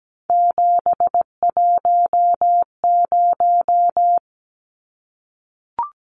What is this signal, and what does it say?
Morse code, heard as a keyed tone, about 17 wpm, 700 Hz. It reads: 710
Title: Numbers - 3-Digit 17wpm